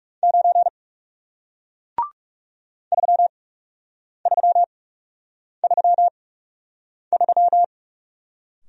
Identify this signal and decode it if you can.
Morse code, heard as a keyed tone, about 40 wpm, spaced as for 8 wpm, 700 Hz. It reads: 9 3333